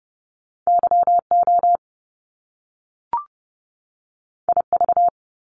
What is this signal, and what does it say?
Morse code, heard as a keyed tone, about 30 wpm, 700 Hz. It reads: YO IV
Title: Top Two Letter Combinations in Words 30wpm